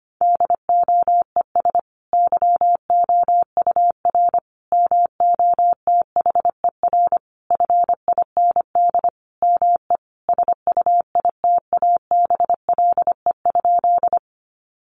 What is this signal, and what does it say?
Morse code, heard as a keyed tone, about 25 wpm, 700 Hz. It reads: DOES YOUR MOTHER FIND ME SUITABLE?
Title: Sentences from Top 1400 Words 25wpm